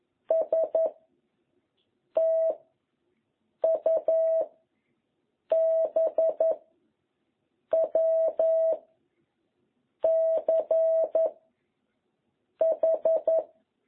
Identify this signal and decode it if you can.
Morse code, heard as a keyed tone, about 11 wpm, 660 Hz. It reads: S T U B W C H